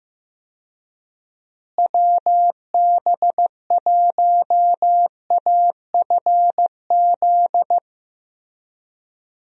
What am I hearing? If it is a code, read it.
Morse code, heard as a keyed tone, about 15 wpm, 700 Hz. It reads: WB1AFZ
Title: USA Call Signs - 6 Characters - Repeated 3x - 15wpm